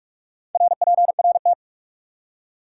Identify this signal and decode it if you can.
Morse code, heard as a keyed tone, about 43 wpm, 700 Hz. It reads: RPRT
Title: Top 180 QSO Elements 45wpm